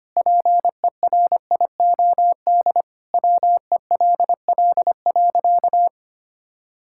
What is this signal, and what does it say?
Morse code, heard as a keyed tone, about 25 wpm, 700 Hz. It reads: PERIOD WELL.
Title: Sentences from Top 1000 Words - Review Part 5 - 25wpm